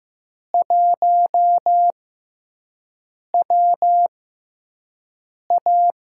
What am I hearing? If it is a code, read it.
Morse code, heard as a keyed tone, about 15 wpm, spaced as for 5 wpm, 700 Hz. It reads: 1WA
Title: USA Call Signs - 4 Characters - 6x Character Spacing - 15wpm